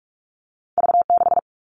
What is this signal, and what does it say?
Morse code, heard as a keyed tone, about 46 wpm, 700 Hz. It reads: V6
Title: All Cut Numbers - 2-Digit 45wpm